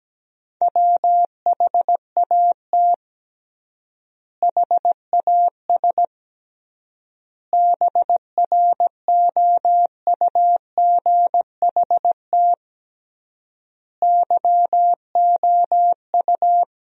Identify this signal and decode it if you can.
Morse code, heard as a keyed tone, about 17 wpm, 700 Hz. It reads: WHAT HAS BROUGHT YOU
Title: 3x Word-Spacing - Sentences from Top 500 Words - 17wpm